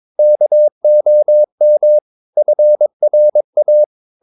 Morse code, heard as a keyed tone, about 22 wpm, 600 Hz. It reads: KOM FRA
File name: Ferie QRS 22wpm